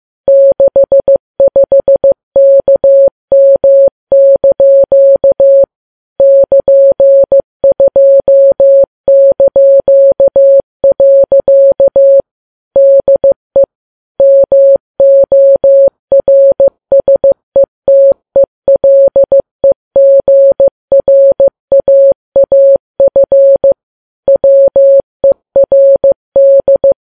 Morse code, heard as a keyed tone, about 15 wpm, 550 Hz. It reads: 65KM) <KN>2). DE MORSETELEGRAAF WERD
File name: Files Web tls / Morse on9fv / morse15kps